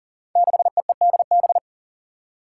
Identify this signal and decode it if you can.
Morse code, heard as a keyed tone, about 40 wpm, 700 Hz. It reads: 6EEDB
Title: Five Challenging Letter-Number Combinations 40wpm